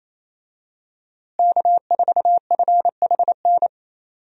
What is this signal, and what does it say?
Morse code, heard as a keyed tone, about 28 wpm, 700 Hz. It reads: K4FHN
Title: USA Call Signs - 5 Characters - Repeated 2x - 28wpm